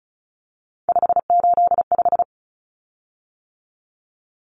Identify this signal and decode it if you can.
Morse code, heard as a keyed tone, about 34 wpm, 700 Hz. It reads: 585
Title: Common Cut Numbers - 3-Digit 35wpm